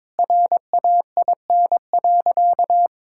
Morse code, heard as a keyed tone, about 23 wpm, 700 Hz. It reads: RAIN.
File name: practice / 22wpm / Sentences with 2-Word Phrases - 22wpm